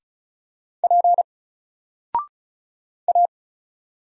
Morse code, heard as a keyed tone, about 35 wpm, 700 Hz. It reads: P A
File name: Single-Letter-Number-35wpm